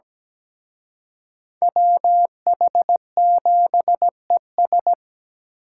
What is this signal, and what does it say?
Morse code, heard as a keyed tone, about 20 wpm, 700 Hz. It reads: WH7ES
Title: Speed Racing - USA 2x2 Call Signs 17wpm